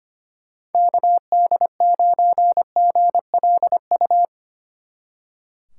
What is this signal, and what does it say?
Morse code, heard as a keyed tone, about 30 wpm, 700 Hz. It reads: KD9GLU
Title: Speed Racing - USA 2x3 Call Signs 25wpm